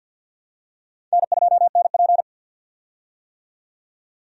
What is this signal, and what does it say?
Morse code, heard as a keyed tone, about 53 wpm, 700 Hz. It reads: N2NP
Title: USA Call Signs - 4 Characters - 50wpm